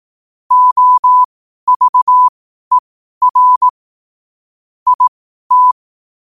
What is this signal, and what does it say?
Morse code, heard as a keyed tone, about 18 wpm, spaced as for 12 wpm, 1000 Hz. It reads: OVER IT
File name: 12wpm